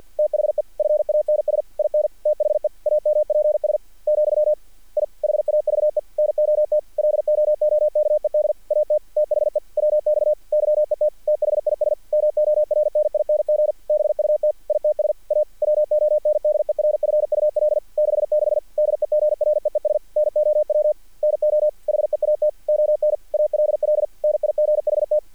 Morse code, heard as a keyed tone, about 48 wpm, 600 Hz. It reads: THE FANS IN THE AMPS , I HAVE NOT LOOKED AT THE WX YET THIS MORNING BUT ITS A WONDERFUL 66 DEGREES NOW NO HEAT ON ALL NIGHT
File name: archived / CRSnet